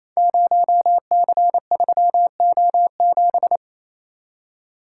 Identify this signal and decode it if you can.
Morse code, heard as a keyed tone, about 29 wpm, 700 Hz. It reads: 0C3O7
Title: Code Groups - 5 Letter:Numbers - 28wpm